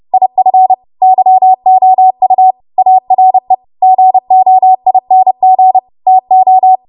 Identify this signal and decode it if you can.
Morse code, heard as a keyed tone, about 30 wpm, 750 Hz. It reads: IF YOU ARE GOING TO
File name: Morse / 30WPM